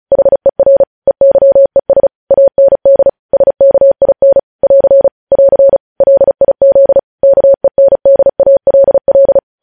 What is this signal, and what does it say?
Morse code, heard as a keyed tone, about 36 wpm, 550 Hz. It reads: HER EYES AND SKIN <AR> <AR> LIZ KENDALL
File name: podcast / News headlines-1x-35x35